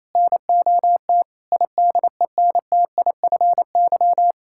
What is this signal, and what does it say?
Morse code, heard as a keyed tone, about 28 wpm, 700 Hz. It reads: NOT IDENTIFY
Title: Sentences from Top 1000 Words - Review Part 6 - 28wpm